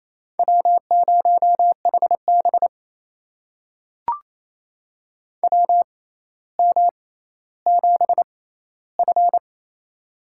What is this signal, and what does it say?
Morse code, heard as a keyed tone, about 28 wpm, spaced as for 14 wpm, 700 Hz. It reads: W0HB WM7F
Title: USA Call Signs - 4 Characters - 6x Character Spacing - 28wpm